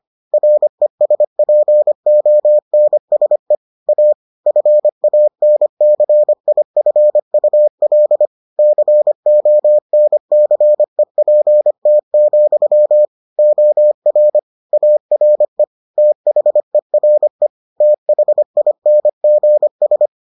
Morse code, heard as a keyed tone, about 25 wpm, 600 Hz. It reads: RESPONSE A FANCIFUL CONCEPT, OR ARE THERE THINGS